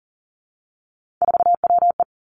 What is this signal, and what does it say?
Morse code, heard as a keyed tone, about 40 wpm, 700 Hz. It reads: 4WE
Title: Three Challenging Letter-Number Combinations 40wpm